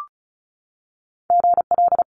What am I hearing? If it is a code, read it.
Morse code, heard as a keyed tone, about 36 wpm, 700 Hz. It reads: GL
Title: Multi-letter Phonemes 35wpm